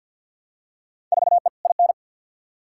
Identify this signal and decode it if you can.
Morse code, heard as a keyed tone, about 51 wpm, 700 Hz. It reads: 4E IN
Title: Field Day Exchanges - 50wpm